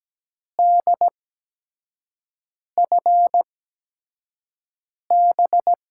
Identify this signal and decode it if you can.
Morse code, heard as a keyed tone, about 17 wpm, 700 Hz. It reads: D F B
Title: USA Call Signs - 8x Character Spacing - 17wpm